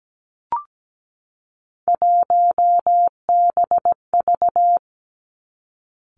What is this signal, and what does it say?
Morse code, heard as a keyed tone, about 17 wpm, 700 Hz. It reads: 1BV
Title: Three Challenging Letter-Number Combinations 17wpm